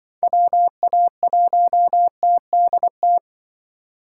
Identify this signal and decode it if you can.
Morse code, heard as a keyed tone, about 24 wpm, 700 Hz. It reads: WA1TDT
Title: Speed Racing - USA 2x3 Call Signs 20wpm